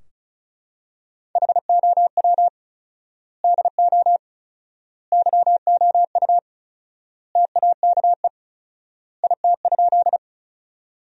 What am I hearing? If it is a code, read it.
Morse code, heard as a keyed tone, about 34 wpm, 700 Hz. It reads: HOW DO YOU TAKE IT?
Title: 4x Word-Spacing - Sentences from Top 200 Words - 35wpm